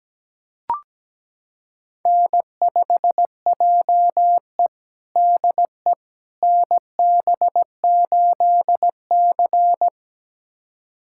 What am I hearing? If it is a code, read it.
Morse code, heard as a keyed tone, about 17 wpm, 700 Hz. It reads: N5JE DE NB8C
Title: USA Callsign DE Callsign - 4 Characters - 17wpm